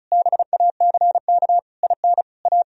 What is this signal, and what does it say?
Morse code, heard as a keyed tone, about 37 wpm, 700 Hz. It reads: BACK IN A
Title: Speed-Racing - Sets of 5 Words 30wpm